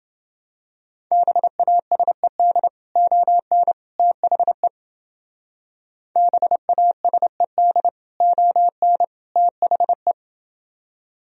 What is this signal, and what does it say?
Morse code, heard as a keyed tone, about 30 wpm, 700 Hz. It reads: BASED ON THE BASED ON THE